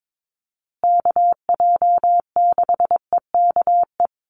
Morse code, heard as a keyed tone, about 22 wpm, 700 Hz. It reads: KJ6EKE
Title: USA Call Signs Repeated 2x 22wpm